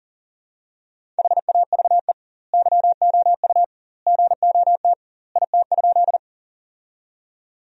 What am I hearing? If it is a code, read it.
Morse code, heard as a keyed tone, about 37 wpm, 700 Hz. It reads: HAVE YOU GOT IT?
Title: 2x Word-Spacing - Sentences from Top 300 Words - 40wpm